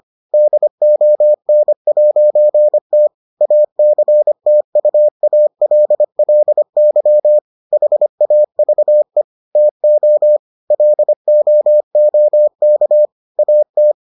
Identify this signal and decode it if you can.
Morse code, heard as a keyed tone, about 25 wpm, 600 Hz. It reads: DON'T ACTUALLY HAVE TO LOOK AT